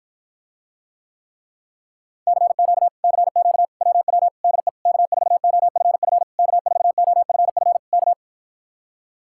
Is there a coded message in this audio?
Yes: Morse code from 2.3 to 8.1 s, heard as a keyed tone, about 53 wpm, 700 Hz. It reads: CQ CQ WW DE K4OFF K4OFF K